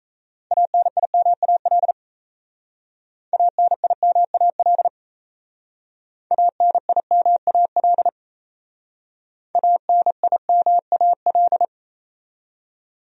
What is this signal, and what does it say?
Morse code, heard as a keyed tone, about 36 wpm, 700 Hz. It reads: ANIMAL ANIMAL ANIMAL ANIMAL